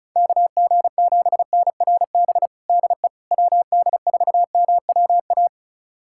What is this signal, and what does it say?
Morse code, heard as a keyed tone, about 36 wpm, 700 Hz. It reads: KG7NRB DE WD4MWA
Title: USA Callsign DE Callsign 35wpm